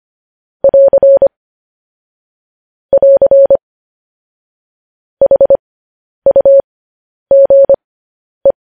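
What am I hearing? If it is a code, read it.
Morse code, heard as a keyed tone, about 25 wpm, spaced as for 10 wpm, 550 Hz. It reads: <AR> <AR> HUGE
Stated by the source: Morse code news headlines broadcast.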